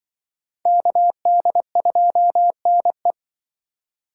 Morse code, heard as a keyed tone, about 26 wpm, 700 Hz. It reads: KD2NE